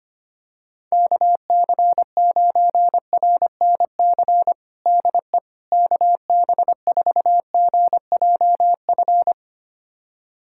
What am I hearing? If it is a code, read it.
Morse code, heard as a keyed tone, about 25 wpm, 700 Hz. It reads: KC9RNC DE KB4GJF
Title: USA Callsign DE Callsign 25wpm